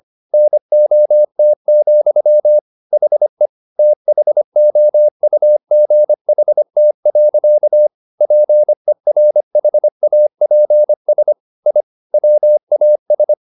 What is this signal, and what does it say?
Morse code, heard as a keyed tone, about 25 wpm, 600 Hz. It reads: NOT, HE THOUGHT. PERHAPS I WAS